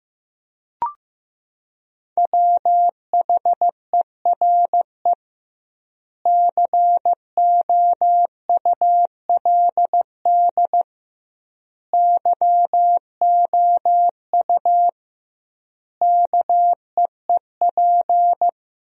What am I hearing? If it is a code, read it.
Morse code, heard as a keyed tone, about 16 wpm, 700 Hz. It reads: WHERE COULD YOU KEEP
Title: 2x Word-Spacing - Sentences from Top 300 Words - 15wpm